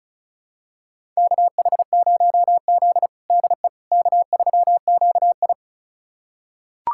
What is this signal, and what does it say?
Morse code, heard as a keyed tone, about 34 wpm, 700 Hz. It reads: KH0Z DE K3QI
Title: USA Callsign DE Callsign - 4 Characters - 35wpm